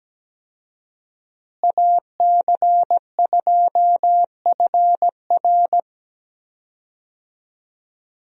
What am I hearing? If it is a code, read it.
Morse code, heard as a keyed tone, about 17 wpm, 700 Hz. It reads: AC2FR
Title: USA Call Signs - 5 Characters - Repeated 3x - 17wpm